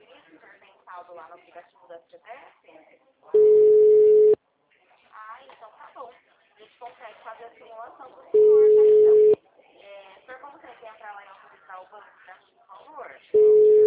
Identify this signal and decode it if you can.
Ringback tone (Europe, 425 Hz), with speech.